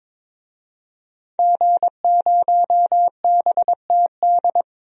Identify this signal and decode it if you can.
Morse code, heard as a keyed tone, about 21 wpm, 700 Hz. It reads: G0BTD
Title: DXCC-100 Call Signs - 5 Characters - 22wpm